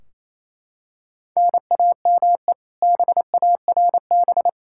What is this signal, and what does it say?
Morse code, heard as a keyed tone, about 28 wpm, 700 Hz. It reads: NAME BARB
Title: K1USN SST QSO Practice 28wpm